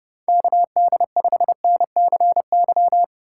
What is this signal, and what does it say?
Morse code, heard as a keyed tone, about 30 wpm, 700 Hz. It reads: KD5NCY